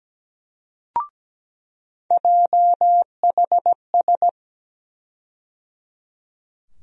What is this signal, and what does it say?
Morse code, heard as a keyed tone, about 17 wpm, 700 Hz. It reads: JHS
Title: USA Call Sign Postfixes - 3 Letters 17wpm